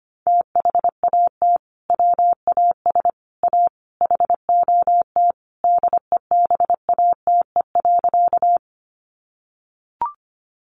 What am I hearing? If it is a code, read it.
Morse code, heard as a keyed tone, about 25 wpm, 690 Hz. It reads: THAT WAS A HOT DEBATE.
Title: Sentences from Top 1100 Words 25wpm